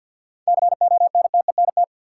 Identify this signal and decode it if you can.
Morse code, heard as a keyed tone, about 52 wpm, 700 Hz. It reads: CONTENT
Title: Sentences from Top 900 Words - New Word Drill - 50wpm